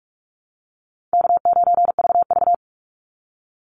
Morse code, heard as a keyed tone, about 44 wpm, 700 Hz. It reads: K9VV
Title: Easy Call Signs - 45wpm